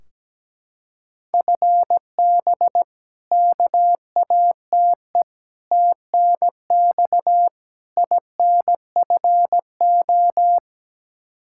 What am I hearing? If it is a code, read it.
Morse code, heard as a keyed tone, about 17 wpm, 700 Hz. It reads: FB KATE TNX INFO
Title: Practice QSO - Exchange 3 - More Info - 17wpm